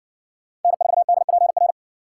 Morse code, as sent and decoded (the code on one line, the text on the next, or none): -. ....- -.. .--. .-.
N4DPR